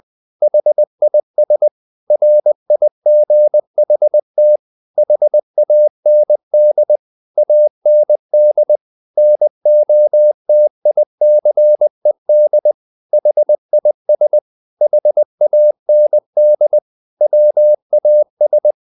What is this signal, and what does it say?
Morse code, heard as a keyed tone, about 20 wpm, 600 Hz. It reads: HIS RIGHT HAND AND NOTICED HIS HAND WAS
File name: the old man and the sea / the old man and the sea part 1 20wpm